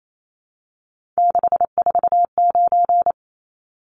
Morse code, heard as a keyed tone, about 28 wpm, 700 Hz. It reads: 649
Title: Numbers - 3-Digit 28wpm